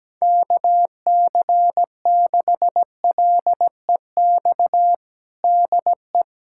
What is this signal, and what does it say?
Morse code, heard as a keyed tone, about 17 wpm, 700 Hz. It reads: KC6LEX DE
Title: USA Callsign DE Callsign 17wpm